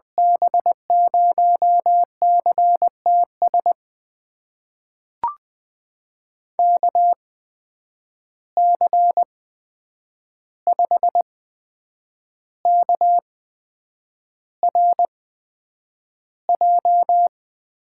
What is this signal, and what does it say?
Morse code, heard as a keyed tone, about 21 wpm, spaced as for 8 wpm, 700 Hz. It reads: B0CTS KC5KRJ
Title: USA Call Signs - 6 Characters - 8x Character Spacing - 20wpm